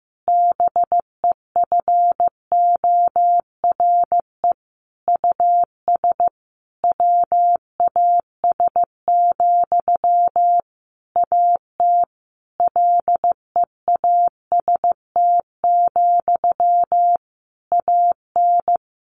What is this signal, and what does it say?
Morse code, heard as a keyed tone, about 15 wpm, 700 Hz. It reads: BEFORE US WAS, AT LEAST, AN
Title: Sentences from Top 1400 Words 15wpm